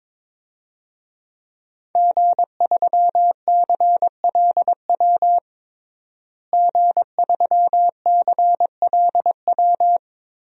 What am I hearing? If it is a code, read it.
Morse code, heard as a keyed tone, about 22 wpm, 700 Hz. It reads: G3CLW G3CLW
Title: DXCC-100 Call Signs - 5 Characters - Repeated 3x - 22wpm